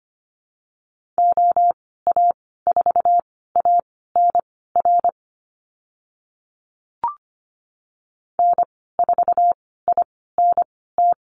Morse code, heard as a keyed tone, about 25 wpm, spaced as for 16 wpm, 700 Hz. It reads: OA4ANR N4INT
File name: Lesson 15g - Number 5 - Call Signs 25wpm